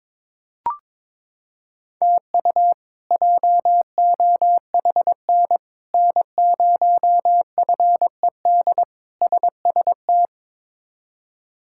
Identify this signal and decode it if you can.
Morse code, heard as a keyed tone, about 22 wpm, 700 Hz. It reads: TU JOHN N0FED SST
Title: K1USN SST QSO Practice - 22wpm